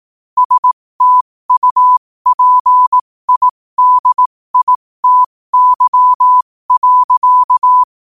Morse code, heard as a keyed tone, about 18 wpm, spaced as for 15 wpm, 1000 Hz. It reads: STUPIDITY.
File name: qotdfiles / 15wpm